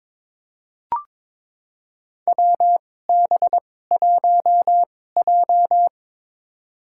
Morse code, heard as a keyed tone, about 21 wpm, spaced as for 16 wpm, 700 Hz. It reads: WB1J
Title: USA Call Signs - 4 Characters - 2x Character Spacing - 22wpm